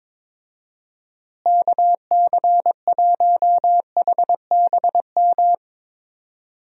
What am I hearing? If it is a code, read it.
Morse code, heard as a keyed tone, about 22 wpm, 700 Hz. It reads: KC1HBM
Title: USA Call Signs - 6 Characters - Repeated 3x - 22wpm